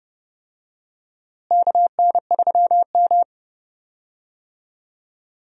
Morse code, heard as a keyed tone, about 29 wpm, 700 Hz. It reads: KN3M
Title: USA Call Signs - 4 Characters - Repeated 2x - 30wpm